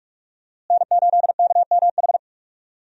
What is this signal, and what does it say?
Morse code, heard as a keyed tone, about 46 wpm, 700 Hz. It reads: N8KMH
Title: USA Call Signs - 5 Characters - Repeated 3x - 45wpm